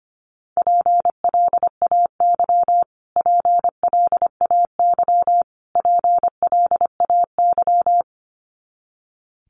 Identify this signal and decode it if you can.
Morse code, heard as a keyed tone, about 25 wpm, 700 Hz. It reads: PLAY PLAY PLAY
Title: Top 500 Words Repeated 25wpm